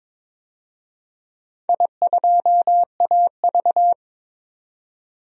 Morse code, heard as a keyed tone, about 21 wpm, 690 Hz. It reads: I2AV
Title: DXCC-100 Call Signs - 4 Characters - 22wpm